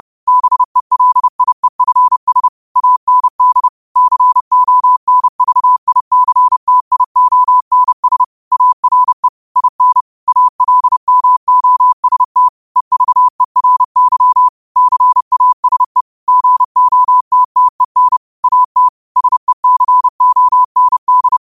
Morse code, heard as a keyed tone, about 30 wpm, 1000 Hz. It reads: BELIEFS AND CONVICTIONS ARE IN ALMOST EVERY CASE GOTTEN AT SECOND